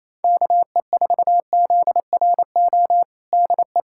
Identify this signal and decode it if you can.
Morse code, heard as a keyed tone, about 28 wpm, 700 Hz. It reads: KE4ZRO DE
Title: USA Callsign DE Callsign 28wpm